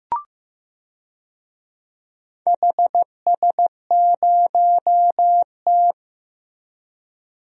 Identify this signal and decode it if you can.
Morse code, heard as a keyed tone, about 15 wpm, 700 Hz. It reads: HS0T
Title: DXCC-100 Call Signs - 4 Characters - Repeated 3x - 15wpm